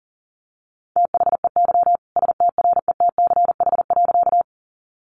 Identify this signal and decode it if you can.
Morse code, heard as a keyed tone, about 40 wpm, 700 Hz. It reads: THEY STRETCH.
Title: Easy Phrases 40wpm